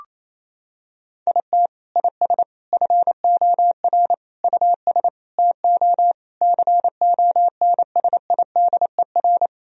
Morse code, heard as a keyed tone, about 28 wpm, 700 Hz. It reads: IT IS FOR US TO CONSIDER
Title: Sentences from Top 1000 Words - Review Part 4 - 28wpm